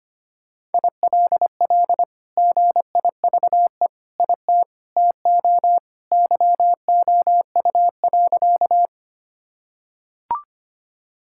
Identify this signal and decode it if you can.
Morse code, heard as a keyed tone, about 25 wpm, 700 Hz. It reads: ILL GIVE IT TO YOU.
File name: practice / 25wpm / Sentences with 2-Word Phrases - 25wpm